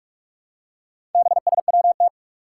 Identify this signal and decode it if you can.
Morse code, heard as a keyed tone, about 44 wpm, 700 Hz. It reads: BSWT
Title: Four Challenging Letter-Number Combinations 45wpm